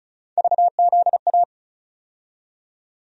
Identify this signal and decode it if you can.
Morse code, heard as a keyed tone, about 34 wpm, 700 Hz. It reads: VZA
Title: USA Call Sign Postfixes - 3 Letters 35wpm